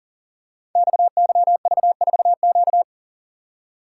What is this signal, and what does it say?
Morse code, heard as a keyed tone, about 39 wpm, 700 Hz. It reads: XYV4Q
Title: Code Groups - 5 Letter:Numbers - 40wpm